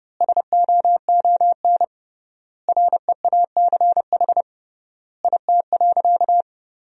Morse code, heard as a keyed tone, about 30 wpm, 700 Hz. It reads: SOON REACH IT.
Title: 3x Word-Spacing - Sentences from Top 400 Words - 30wpm